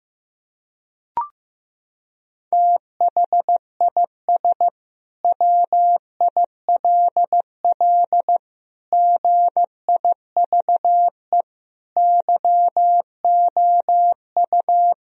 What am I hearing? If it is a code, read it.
Morse code, heard as a keyed tone, about 15 wpm, 700 Hz. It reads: THIS WILL GIVE YOU
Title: Sentences from Top 1000 Words - Review Part 3 - 15wpm